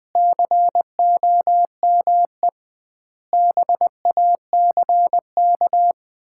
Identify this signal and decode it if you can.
Morse code, heard as a keyed tone, about 20 wpm, 700 Hz. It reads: COME BACK
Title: 2x Word-Spacing - Sets of 5 Words - 20wpm